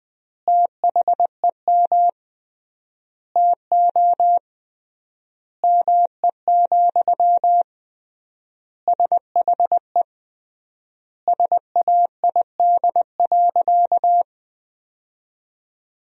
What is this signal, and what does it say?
Morse code, heard as a keyed tone, about 20 wpm, 700 Hz. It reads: THEM TO ME, SHE SAID.
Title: 3x Word-Spacing - Sentences from Top 500 Words - 20wpm